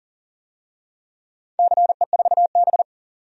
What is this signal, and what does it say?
Morse code, heard as a keyed tone, about 40 wpm, 700 Hz. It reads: CE4B
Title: DXCC-100 Call Signs - 4 Characters - 40wpm